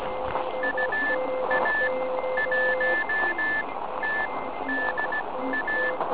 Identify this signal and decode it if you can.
Morse code, heard as a keyed tone, about 17 wpm, spaced as for 12 wpm, 1750 Hz. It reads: UA1TDA
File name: ARCHIV / KX3 / PorovnaniKx3 RM40